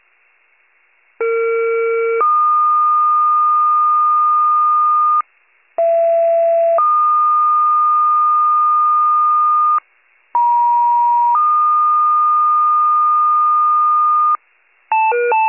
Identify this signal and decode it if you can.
Two-tone page, heard as a one-second tone, then a three-second tone, about 460 then 1220 Hz, about 670 then 1220 Hz, about 950 then 1220 Hz. It is heard from 1.2 to 14.4 s.
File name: pager / 83 RESCUE / wcalert